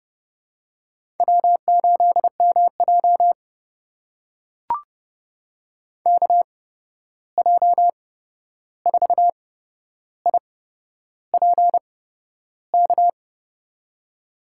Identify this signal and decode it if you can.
Morse code, heard as a keyed tone, about 32 wpm, spaced as for 11 wpm, 700 Hz. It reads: W8MJ KJ4IPK
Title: USA Call Signs - 8x Character Spacing - 30wpm